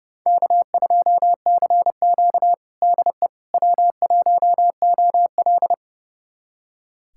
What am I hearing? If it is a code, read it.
Morse code, heard as a keyed tone, about 32 wpm, 700 Hz. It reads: K2CQ DE W1OL
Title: USA Callsign DE Callsign - 4 Characters - 30wpm